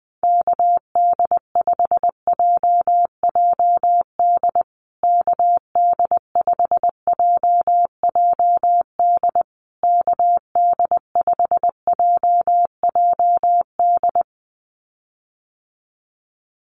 Morse code, heard as a keyed tone, about 20 wpm, 700 Hz. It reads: KD5JJD KD5JJD KD5JJD
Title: USA Call Signs Repeated 3x 20wpm